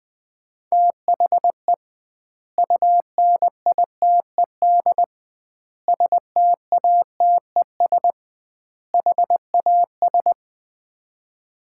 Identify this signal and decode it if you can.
Morse code, heard as a keyed tone, about 20 wpm, 700 Hz. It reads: THE UNITED STATES HAS
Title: 2x Word-Spacing - Sets of 4 Words - 20wpm